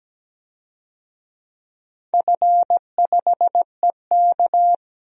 Morse code, heard as a keyed tone, about 17 wpm, 700 Hz. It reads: F5EK
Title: DXCC-100 Call Signs - 4 Characters - 17wpm